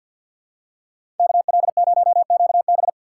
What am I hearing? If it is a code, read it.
Morse code, heard as a keyed tone, about 53 wpm, 700 Hz. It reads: KL0QB
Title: USA Call Signs - 5 Characters - 4x Character Spacing - 50wpm